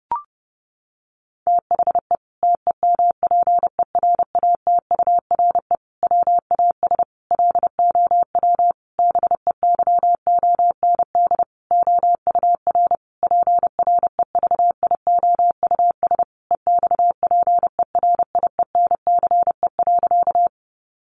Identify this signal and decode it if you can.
Morse code, heard as a keyed tone, about 30 wpm, 700 Hz. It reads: THE TEMPERATURE WAS LOW BEYOND OUR PREVIOUS EXPERIENCE.
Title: Sentences from Top 1500 Words 30wpm